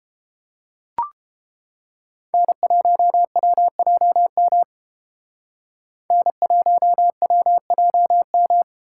Morse code, heard as a keyed tone, about 31 wpm, 700 Hz. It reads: N1WJM N1WJM